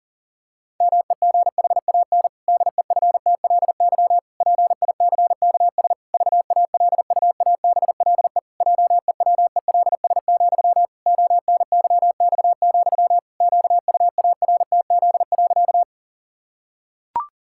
Morse code, heard as a keyed tone, about 40 wpm, 700 Hz. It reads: MEGHAN DEFTLY PICKS VALUABLE JEWELS, ONYX, QUARTZ.
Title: Pangram Challenge v2 - 40wpm